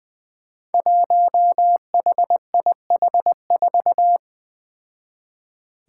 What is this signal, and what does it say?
Morse code, heard as a keyed tone, about 21 wpm, 700 Hz. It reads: 1HIH4
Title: Five Challenging Letter-Number Combinations 20wpm